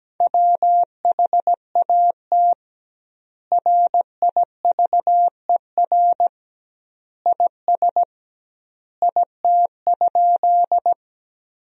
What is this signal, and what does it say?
Morse code, heard as a keyed tone, about 17 wpm, 700 Hz. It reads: WHAT RIVER IS IT?
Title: 2x Word-Spacing - Sentences from Top 300 Words - 17wpm